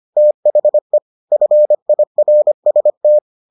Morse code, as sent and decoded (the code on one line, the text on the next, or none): - .... / ..-. .. .-. ... -
THE FIRST